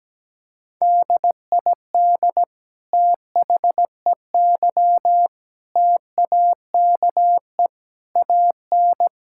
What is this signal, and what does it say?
Morse code, heard as a keyed tone, about 17 wpm, 700 Hz. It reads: DID THEY TAKE AN
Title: Sentences from Top 1000 Words - Review Part 6 - 17wpm